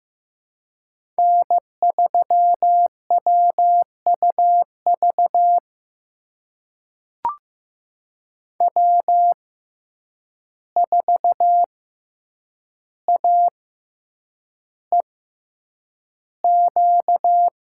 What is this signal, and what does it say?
Morse code, heard as a keyed tone, about 15 wpm, spaced as for 8 wpm, 700 Hz. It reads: N3WUV W4AEQ
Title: USA Call Signs - 5 Characters - 6x Character Spacing - 15wpm